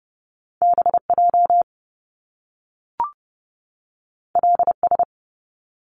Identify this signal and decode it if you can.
Morse code, heard as a keyed tone, about 30 wpm, 700 Hz. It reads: BJ LS
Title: Top Two Letter Combinations in Words 30wpm